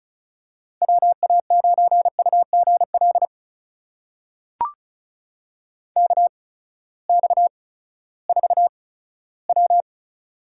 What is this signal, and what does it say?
Morse code, heard as a keyed tone, about 34 wpm, spaced as for 17 wpm, 700 Hz. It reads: WA9UGL KX4W
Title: USA Call Signs - 8x Character Spacing - 35wpm